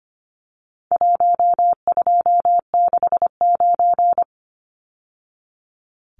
Morse code, heard as a keyed tone, about 25 wpm, 700 Hz. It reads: 1269
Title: Numbers - Contest Style - 4-Digit 25wpm